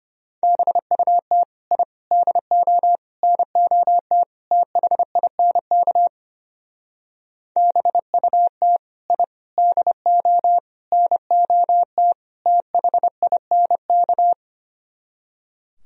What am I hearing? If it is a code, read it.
Morse code, heard as a keyed tone, about 32 wpm, 700 Hz. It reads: BUT I DO NOT THINK BUT I DO NOT THINK